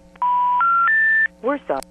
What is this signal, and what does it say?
Special information tone, with speech.
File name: TurtleCreekEWSD